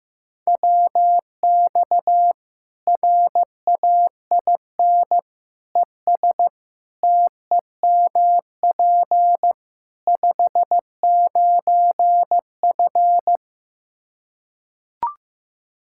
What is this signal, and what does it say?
Morse code, heard as a keyed tone, about 15 wpm, 700 Hz. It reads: WX RAIN ES TEMP 59F
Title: Practice QSO - Exchange 2 - Info - 15wpm